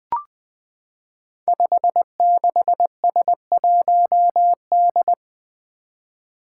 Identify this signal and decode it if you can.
Morse code, heard as a keyed tone, about 20 wpm, 700 Hz. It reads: 56S1D
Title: Five Challenging Letter-Number Combinations 20wpm